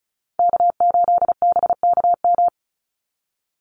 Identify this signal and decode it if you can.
Morse code, heard as a keyed tone, about 34 wpm, 700 Hz. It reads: K8BKM
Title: Call Signs Repeated 2x - 35wpm